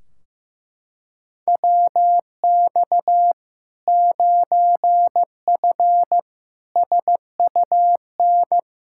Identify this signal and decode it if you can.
Morse code, heard as a keyed tone, about 15 wpm, 700 Hz. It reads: WX 9F SUN
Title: Practice QSO - Exchange 2 - Info - 15wpm